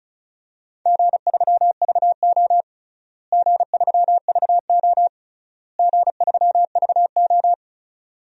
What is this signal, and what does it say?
Morse code, heard as a keyed tone, about 36 wpm, 700 Hz. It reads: G3VO G3VO G3VO
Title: DXCC-100 Call Signs - 4 Characters - Repeated 3x - 35wpm